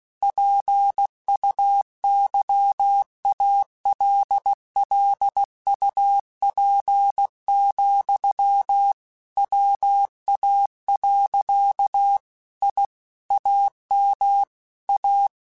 Morse code, heard as a keyed tone, about 16 wpm, 800 Hz. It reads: PUYALLUP, WA. I AM A
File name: QSOs / QSOs-mp3s / kc7haa@15